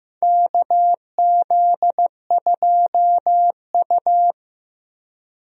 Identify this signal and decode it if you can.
Morse code, heard as a keyed tone, about 15 wpm, 700 Hz. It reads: KZ2U
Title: USA Call Signs - 4 Characters - Repeated 2x - 15wpm